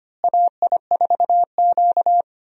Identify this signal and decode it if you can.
Morse code, heard as a keyed tone, about 25 wpm, 700 Hz. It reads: AI4Q
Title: USA Call Signs - 4 Characters - Repeated 2x - 25wpm